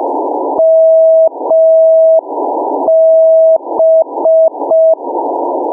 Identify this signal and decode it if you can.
Morse code, heard as a keyed tone, about 5 wpm, 650 Hz. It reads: MB
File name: mw / 0437 499 mb beacon